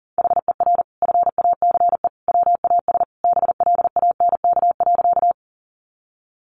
Morse code, heard as a keyed tone, about 40 wpm, 700 Hz. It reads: HER FACE WAS BLANK.
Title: Sentences from Top 1500 Words 40wpm